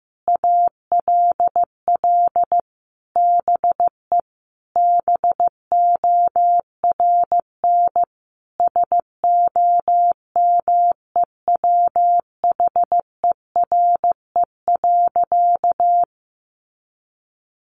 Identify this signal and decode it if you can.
Morse code, heard as a keyed tone, about 15 wpm, 700 Hz. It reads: ALL BE BORN SOMEWHERE.
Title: Sentences from Top 1400 Words 15wpm